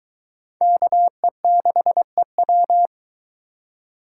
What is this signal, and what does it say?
Morse code, heard as a keyed tone, about 22 wpm, 700 Hz. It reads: KE6EW